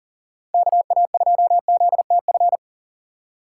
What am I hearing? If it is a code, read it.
Morse code, heard as a keyed tone, about 40 wpm, 700 Hz. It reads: KA2ZTF
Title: USA Call Signs - 6 Characters - 8x Character Spacing - 40wpm